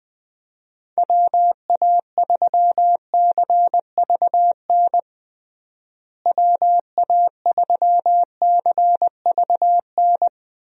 Morse code, heard as a keyed tone, about 21 wpm, 700 Hz. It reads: WA3CVN WA3CVN
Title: USA Call Signs - 6 Characters - Repeated 3x - 20wpm